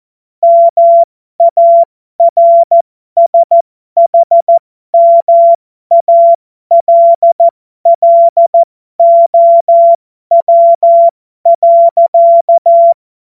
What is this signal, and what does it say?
Morse code, heard as a keyed tone, about 14 wpm, spaced as for 12 wpm, 680 Hz. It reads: MARSHMALLOW.
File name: Morsekurs / Witze